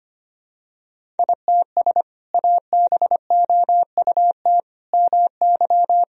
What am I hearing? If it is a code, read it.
Morse code, heard as a keyed tone, about 25 wpm, 700 Hz. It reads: ITS ABOUT MY
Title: Sentences from Top 1000 Words - Review Part 1 - 25wpm